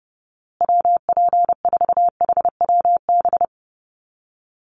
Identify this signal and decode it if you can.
Morse code, heard as a keyed tone, about 30 wpm, 700 Hz. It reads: WP4HWB
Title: USA Call Signs Repeated 2x 30wpm